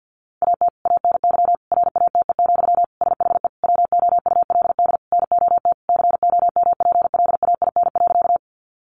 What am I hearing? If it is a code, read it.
Morse code, heard as a keyed tone, about 50 wpm, 700 Hz. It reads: AT ANY RATE, SHE WOULD NOT COMPLAIN.
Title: Sentences from Top 1300 Words 50wpm